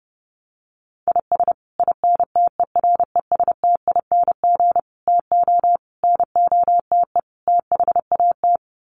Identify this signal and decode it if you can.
Morse code, heard as a keyed tone, about 30 wpm, 700 Hz. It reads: IS INTERESTING TO NOTE THAT
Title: Sets of 5 Words Encore2 30wpm